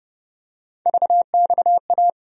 Morse code, heard as a keyed tone, about 31 wpm, 700 Hz. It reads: VXA
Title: USA Call Sign Postfixes - 3 Letters 30wpm